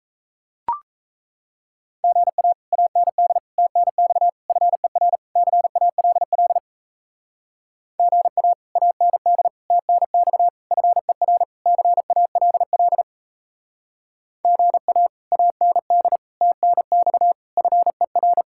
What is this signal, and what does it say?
Morse code, heard as a keyed tone, about 33 wpm, 700 Hz. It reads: GA AND TNX FER CALL GA AND TNX FER CALL GA AND TNX FER
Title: Speed-Racing Common QSO Phrases - 28wpm